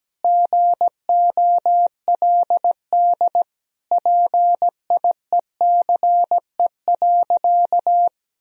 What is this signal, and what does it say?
Morse code, heard as a keyed tone, about 17 wpm, 700 Hz. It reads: GOLD PIECE.
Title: Sentences from Top 1000 Words - Review Part 3 - 17wpm